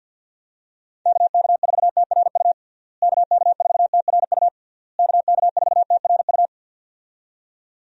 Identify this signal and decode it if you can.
Morse code, heard as a keyed tone, about 46 wpm, 700 Hz. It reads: KK4TRU KK4TRU KK4TRU
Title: USA Call Signs - 6 Characters - Repeated 3x - 50wpm